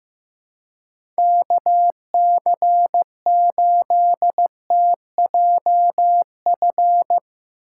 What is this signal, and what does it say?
Morse code, heard as a keyed tone, about 15 wpm, 700 Hz. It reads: KC8TJF
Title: USA Call Signs - 6 Characters - 2x Character Spacing - 15wpm